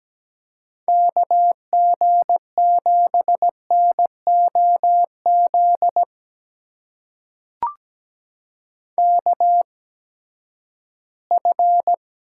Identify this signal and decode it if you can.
Morse code, heard as a keyed tone, about 17 wpm, spaced as for 12 wpm, 700 Hz. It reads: KG7NOZ KF
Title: USA Call Signs - 6 Characters - 8x Character Spacing - 17wpm